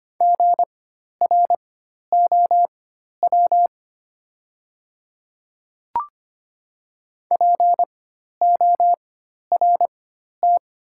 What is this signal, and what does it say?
Morse code, heard as a keyed tone, about 25 wpm, spaced as for 12 wpm, 700 Hz. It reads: GROW PORT
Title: 4x Character Spacing - Top 300 Words - 25wpm